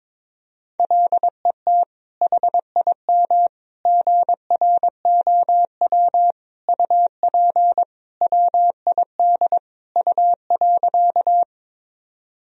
Morse code, heard as a keyed tone, about 22 wpm, 700 Hz. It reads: LET HIM GROW UP WID U.
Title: Sentences from Top 300 Words and Abbreviations 22wpm